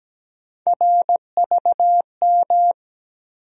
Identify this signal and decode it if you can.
Morse code, heard as a keyed tone, about 17 wpm, 700 Hz. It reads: RVM
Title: USA Call Sign Postfixes - 3 Letters 17wpm